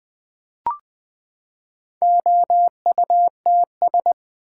Morse code, heard as a keyed tone, about 20 wpm, 700 Hz. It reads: OUTS
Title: Build A Word - 4 Letters At A Time - 11-Letter Words - 20wpm